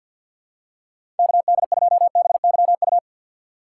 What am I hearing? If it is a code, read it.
Morse code, heard as a keyed tone, about 52 wpm, 700 Hz. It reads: KD2BYU
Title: USA Call Signs - 6 Characters - 8x Character Spacing - 50wpm